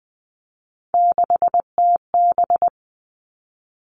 Morse code, heard as a keyed tone, about 19 wpm, 700 Hz. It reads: 6TB
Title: All Cut Numbers - 3-Digit 20wpm